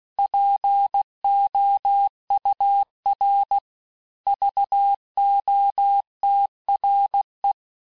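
Morse code, heard as a keyed tone, about 16 wpm, 800 Hz. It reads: POUR VOTRE
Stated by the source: Morse course text, end of QSO, sent at 15 words per minute.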